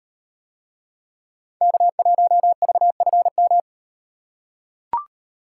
Morse code, heard as a keyed tone, about 38 wpm, 700 Hz. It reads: K1VFM